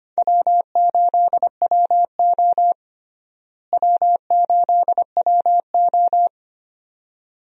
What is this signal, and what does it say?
Morse code, heard as a keyed tone, about 25 wpm, 700 Hz. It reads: W8WO W8WO
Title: USA Call Signs - 4 Characters - Repeated 2x - 25wpm